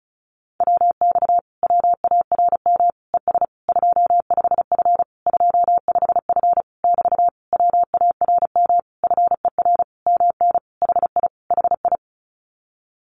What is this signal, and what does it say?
Morse code, heard as a keyed tone, about 34 wpm, 700 Hz. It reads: WX WARM ES 25F 25F <BT> WARM FER MN HI HI
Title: Common QSO Phrases Encore 35wpm